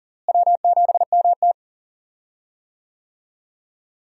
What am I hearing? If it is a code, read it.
Morse code, heard as a keyed tone, about 40 wpm, 700 Hz. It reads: W7MT
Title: USA Call Signs - 4 Characters - Repeated 2x - 40wpm